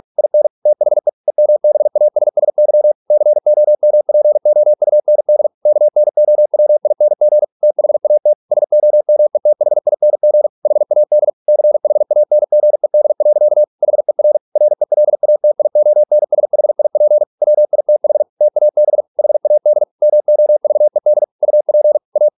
Morse code, heard as a keyed tone, about 46 wpm, 600 Hz. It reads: IN THE EMBASSY COMPOUND KNOWING THAT SOMETHING HAD CHANGED. HER RELATIONSHIP WITH TAB HAD MOVED UP A